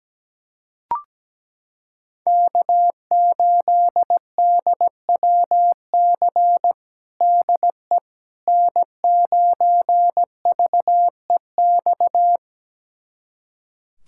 Morse code, heard as a keyed tone, about 17 wpm, 700 Hz. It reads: K8DWC DE N9VEX
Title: USA Callsign DE Callsign - 5 Characters - 17wpm